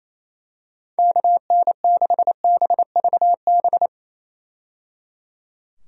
Morse code, heard as a keyed tone, about 28 wpm, 700 Hz. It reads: KN6BVB
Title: USA Call Signs - 6 Characters - 28wpm